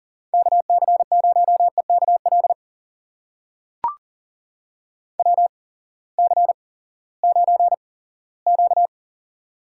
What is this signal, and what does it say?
Morse code, heard as a keyed tone, about 40 wpm, spaced as for 20 wpm, 700 Hz. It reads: KC0EKL WC9Q
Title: USA Call Signs - 8x Character Spacing - 40wpm